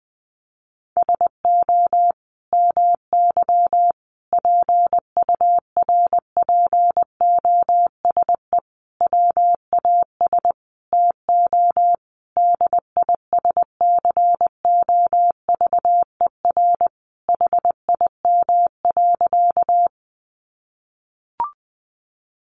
Morse code, heard as a keyed tone, about 20 wpm, 700 Hz. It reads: SO MY PURPOSE WAS TO DISCOVER HIM.
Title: Sentences from Top 1100 Words 20wpm